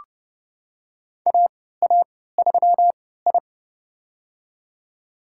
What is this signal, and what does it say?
Morse code, heard as a keyed tone, about 32 wpm, spaced as for 17 wpm, 700 Hz. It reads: AA3I
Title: USA Call Signs - 4 Characters - 3x Character Spacing - 30wpm